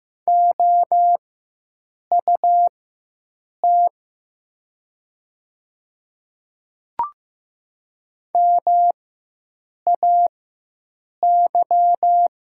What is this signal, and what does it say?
Morse code, heard as a keyed tone, about 15 wpm, spaced as for 7 wpm, 700 Hz. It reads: OUT MAY
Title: 4x Character Spacing - Top 100 Words - 15wpm